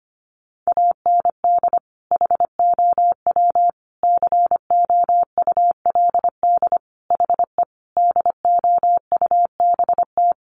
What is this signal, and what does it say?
Morse code, heard as a keyed tone, about 25 wpm, 700 Hz. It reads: AND HOW COULD HE DOUBT
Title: Sentences from Top 1500 Words 25wpm